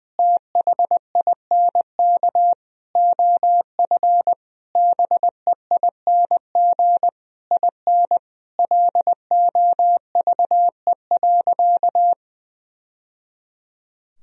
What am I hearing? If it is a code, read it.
Morse code, heard as a keyed tone, about 21 wpm, 700 Hz. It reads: THINK OF BEING IN LOVE.
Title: Sentences from Top 1000 Words - Review Part 3 - 20wpm